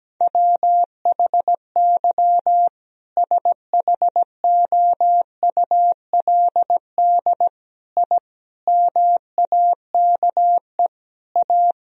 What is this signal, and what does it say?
Morse code, heard as a keyed tone, about 17 wpm, 700 Hz. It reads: WHY SHOULD I MAKE A
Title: Sentences from Top 1000 Words - Review Part 1 - 17wpm